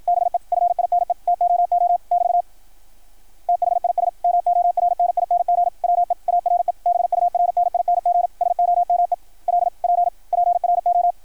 Morse code, heard as a keyed tone, about 53 wpm, 660 Hz. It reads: BE FINE TOO <BT> THIS MORNING WE ARE BURNING SOME LP PRO
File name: archived / CRSnet